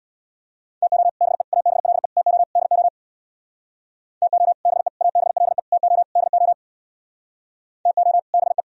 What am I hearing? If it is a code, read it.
Morse code, heard as a keyed tone, about 64 wpm, 690 Hz. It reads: TO BE ABLE TO DO TO BE ABLE TO DO TO BE